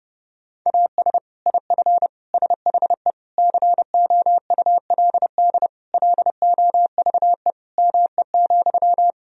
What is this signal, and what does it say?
Morse code, heard as a keyed tone, about 32 wpm, 700 Hz. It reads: AS IF SHE COULD LOVE ME,
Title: Sentences from Top 1000 Words - Review Part 1 - 30wpm